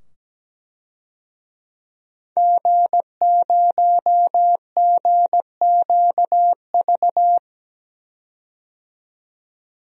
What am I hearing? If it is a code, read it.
Morse code, heard as a keyed tone, about 17 wpm, 700 Hz. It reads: G0GQV
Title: DXCC-100 Call Signs - 5 Characters - 17wpm